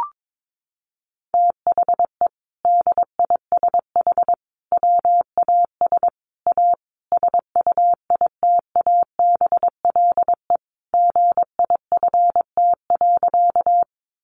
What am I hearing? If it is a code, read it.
Morse code, heard as a keyed tone, about 22 wpm, 700 Hz. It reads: THE DISH WAS A SUITABLE GIFT.
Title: Sentences from Top 1400 Words 22wpm